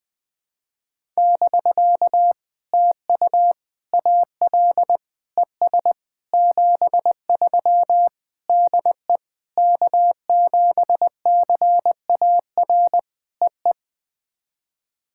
Morse code, heard as a keyed tone, about 20 wpm, 700 Hz. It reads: <BK> TU AL ES 73 DE K7CAR EE
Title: POTA QSO Practice - 20wpm